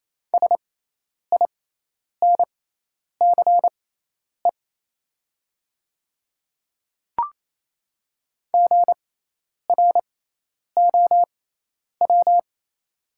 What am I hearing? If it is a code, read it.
Morse code, heard as a keyed tone, about 28 wpm, spaced as for 10 wpm, 700 Hz. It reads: SINCE GROW
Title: 6x Character Spacing - Top 300 Words - 28wpm